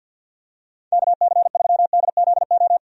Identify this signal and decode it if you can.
Morse code, heard as a keyed tone, about 52 wpm, 700 Hz. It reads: KC3DZO
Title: USA Call Signs - 6 Characters - 50wpm